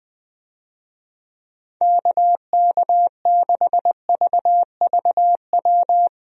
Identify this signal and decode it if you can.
Morse code, heard as a keyed tone, about 20 wpm, 700 Hz. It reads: KK6VVW
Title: USA Call Signs - 6 Characters - 20wpm